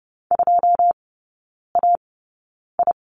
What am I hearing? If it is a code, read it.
Morse code, heard as a keyed tone, about 30 wpm, spaced as for 9 wpm, 690 Hz. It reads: 2AI
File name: Lesson 17e - Letter u - Three Characters 30wpm